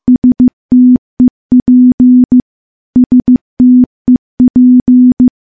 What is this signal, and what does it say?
Morse code, heard as a keyed tone, about 15 wpm, 250 Hz. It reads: STEP STEP
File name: morse-training-15WPM